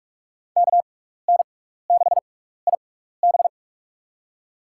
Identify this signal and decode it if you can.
Morse code, heard as a keyed tone, about 47 wpm, spaced as for 16 wpm, 700 Hz. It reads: KN6IB
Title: USA Call Signs - 5 Characters - 6x Character Spacing - 45wpm